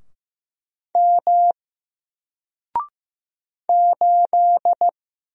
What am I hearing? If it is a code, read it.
Morse code, heard as a keyed tone, about 15 wpm, 700 Hz. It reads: M 8
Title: Single Character (Letter:Number:Punctuation:Prosign) Alt 15wpm